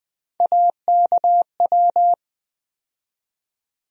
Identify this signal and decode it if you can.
Morse code, heard as a keyed tone, about 20 wpm, 700 Hz. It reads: AKW
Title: USA Call Sign Postfixes - 3 Letters 20wpm